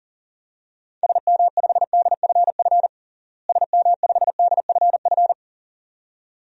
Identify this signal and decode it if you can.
Morse code, heard as a keyed tone, about 40 wpm, 700 Hz. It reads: SM5DFF SM5DFF
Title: DXCC-100 Call Signs - 6 Characters - Repeated 2x - 40wpm